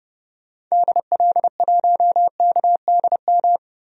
Morse code, heard as a keyed tone, about 32 wpm, 700 Hz. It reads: DL1KDM
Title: DXCC-100 Call Signs - 6 Characters - 30wpm